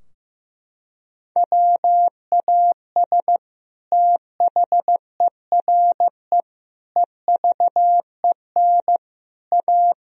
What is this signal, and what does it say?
Morse code, heard as a keyed tone, about 15 wpm, 700 Hz. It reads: WAS THERE EVEN A
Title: Sentences from Top 1000 Words - Review Part 4 - 15wpm